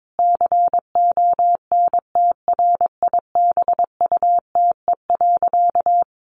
Morse code, heard as a keyed tone, about 22 wpm, 700 Hz. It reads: CONTRIBUTE.
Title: Sentences from Top 1400 Words 22wpm